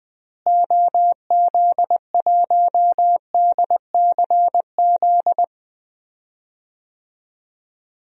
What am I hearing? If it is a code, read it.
Morse code, heard as a keyed tone, about 20 wpm, 700 Hz. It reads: OZ1DCZ
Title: DXCC-100 Call Signs - 6 Characters - Repeated 2x - 20wpm